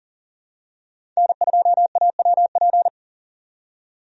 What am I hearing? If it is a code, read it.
Morse code, heard as a keyed tone, about 40 wpm, 700 Hz. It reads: N2AWP
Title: USA Call Signs - 5 Characters - 40wpm